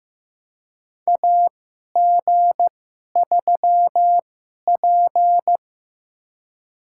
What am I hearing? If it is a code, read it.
Morse code, heard as a keyed tone, about 15 wpm, spaced as for 11 wpm, 700 Hz. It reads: AG3P
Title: USA Call Signs - 2x Character Spacing - 15wpm